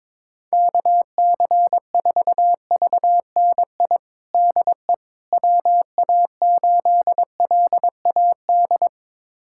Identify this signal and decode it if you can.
Morse code, heard as a keyed tone, about 21 wpm, 700 Hz. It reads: KC4VNI DE WA8LAD
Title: USA Callsign DE Callsign 22wpm